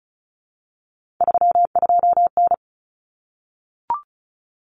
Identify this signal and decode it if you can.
Morse code, heard as a keyed tone, about 34 wpm, 700 Hz. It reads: 32N
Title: Common Cut Numbers - 3-Digit 35wpm